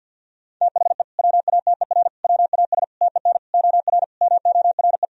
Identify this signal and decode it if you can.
Morse code, heard as a keyed tone, about 53 wpm, 700 Hz. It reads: THE WATER WAS TEN OR MORE
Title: Sentences from Top 1000 Words - Review Part 3 - 50wpm